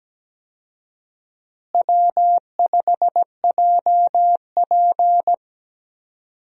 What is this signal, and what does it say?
Morse code, heard as a keyed tone, about 17 wpm, 700 Hz. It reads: W5JP
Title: USA Call Signs - 4 Characters - Repeated 3x - 17wpm